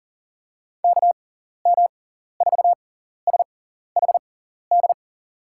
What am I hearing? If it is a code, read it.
Morse code, heard as a keyed tone, about 41 wpm, spaced as for 14 wpm, 700 Hz. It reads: KM4SHD
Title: USA Call Signs - 6 Characters - 6x Character Spacing - 40wpm